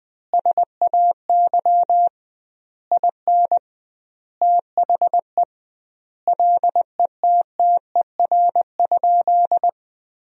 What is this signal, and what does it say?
Morse code, heard as a keyed tone, about 20 wpm, 700 Hz. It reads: SAY IN THE LETTER?
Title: 2x Word-Spacing - Sentences from Top 300 Words - 20wpm